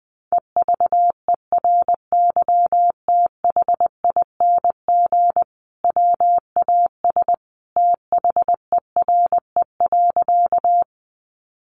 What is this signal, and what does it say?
Morse code, heard as a keyed tone, about 20 wpm, 700 Hz. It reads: EVERYTHING WAS THERE.
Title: Sentences from Top 700 Words 20wpm